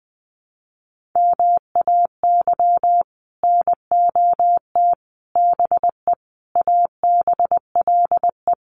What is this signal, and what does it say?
Morse code, heard as a keyed tone, about 19 wpm, 700 Hz. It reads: MAY NOT BE ABLE
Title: Sets of 5 Words 20wpm